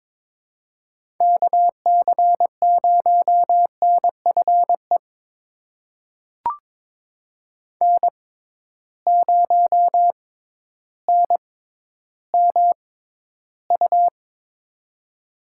Morse code, heard as a keyed tone, about 22 wpm, spaced as for 12 wpm, 700 Hz. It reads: KC0NFE N0NMU
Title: USA Call Signs - 6x Character Spacing - 22wpm